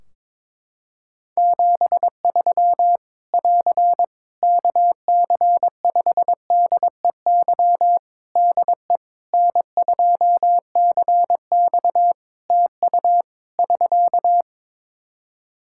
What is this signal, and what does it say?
Morse code, heard as a keyed tone, about 22 wpm, 700 Hz. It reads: 73 <AR> KC5DEY DE N2CX TU <SK>
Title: Practice QSO - Exchange 4 - Ending - 22wpm